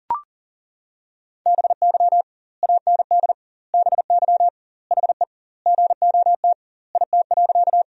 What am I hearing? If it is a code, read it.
Morse code, heard as a keyed tone, about 40 wpm, 700 Hz. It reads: BY AND BY HE GOT IT.
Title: 2x Word-Spacing - Sentences from Top 300 Words - 40wpm